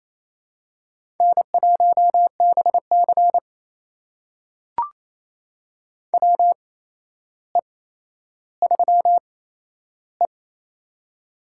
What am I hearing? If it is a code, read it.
Morse code, heard as a keyed tone, about 28 wpm, spaced as for 12 wpm, 700 Hz. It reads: N1BC WE3E
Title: USA Call Signs - 4 Characters - 8x Character Spacing - 28wpm